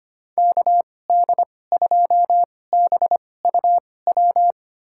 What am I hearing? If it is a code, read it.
Morse code, heard as a keyed tone, about 25 wpm, spaced as for 18 wpm, 700 Hz. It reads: KD2BUW
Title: USA Call Signs - 2x Character Spacing - 25wpm